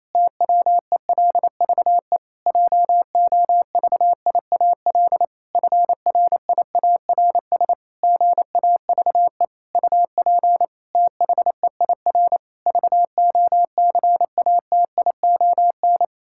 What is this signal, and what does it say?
Morse code, heard as a keyed tone, about 27 wpm, 700 Hz. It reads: TWELVE JOVIAL FRIARS GAVE UP THEIR VOCATION
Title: Pangram Challenge v2 - 28wpm